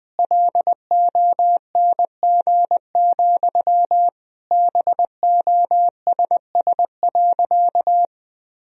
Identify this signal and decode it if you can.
Morse code, heard as a keyed tone, about 20 wpm, 700 Hz. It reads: LONG, BOSS.
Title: Sentences from Top 1000 Words - Review Part 4 - 20wpm